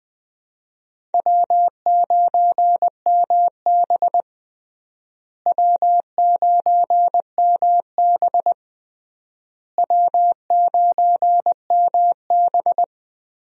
Morse code, heard as a keyed tone, about 21 wpm, 700 Hz. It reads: W9MB W9MB W9MB
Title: USA Call Signs - 4 Characters - Repeated 3x - 20wpm